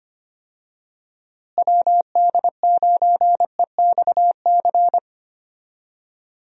Morse code, heard as a keyed tone, about 25 wpm, 700 Hz. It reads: WD9EXC
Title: USA Call Signs - 6 Characters - Repeated 3x - 25wpm